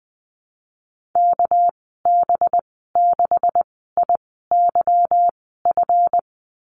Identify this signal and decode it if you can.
Morse code, heard as a keyed tone, about 20 wpm, spaced as for 15 wpm, 700 Hz. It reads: KB6IYF
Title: 207 Lesson 33g - Letter j - Call Signs 20wpm